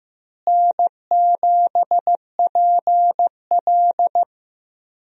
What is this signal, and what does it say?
Morse code, heard as a keyed tone, about 14 wpm, 700 Hz. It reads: N7PL
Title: USA Call Signs - 4 Characters - Repeated 3x - 15wpm